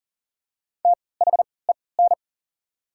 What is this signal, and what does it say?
Morse code, heard as a keyed tone, about 40 wpm, spaced as for 23 wpm, 690 Hz. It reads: THEN
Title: 3x Character Spacing - Top 100 Words - 40wpm